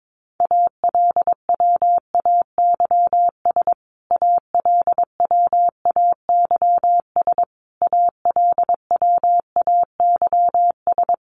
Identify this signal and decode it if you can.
Morse code, heard as a keyed tone, about 22 wpm, 700 Hz. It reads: ALWAYS ALWAYS ALWAYS
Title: Top 500 Words Repeated 22wpm